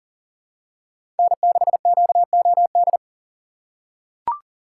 Morse code, heard as a keyed tone, about 40 wpm, 700 Hz. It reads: N6QOD
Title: USA Call Signs - 2x Character Spacing - 40wpm